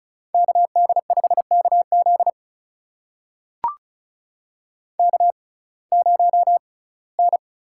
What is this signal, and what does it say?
Morse code, heard as a keyed tone, about 34 wpm, spaced as for 21 wpm, 700 Hz. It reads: KD5KZ K0N
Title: USA Call Signs - 6x Character Spacing - 35wpm